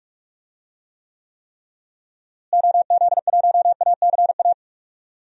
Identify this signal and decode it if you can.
Morse code, heard as a keyed tone, about 46 wpm, 700 Hz. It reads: OZ1ACA
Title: DXCC-100 Call Signs - 6 Characters - Repeated 3x - 45wpm